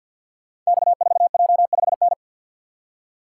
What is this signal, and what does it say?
Morse code, heard as a keyed tone, about 52 wpm, 700 Hz. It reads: X4J5N
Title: Code Groups - 5 Letter:Numbers - 50wpm